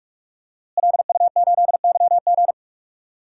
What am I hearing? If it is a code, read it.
Morse code, heard as a keyed tone, about 42 wpm, 700 Hz. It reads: LU8YG
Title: DXCC-100 Call Signs - 5 Characters - Repeated 3x - 45wpm